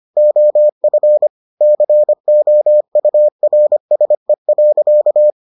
Morse code, heard as a keyed tone, about 25 wpm, 600 Hz. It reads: OF COURSE.